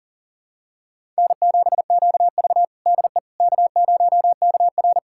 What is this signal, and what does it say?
Morse code, heard as a keyed tone, about 37 wpm, 700 Hz. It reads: N7QV DE K0KR
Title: USA Callsign DE Callsign - 4 Characters - 40wpm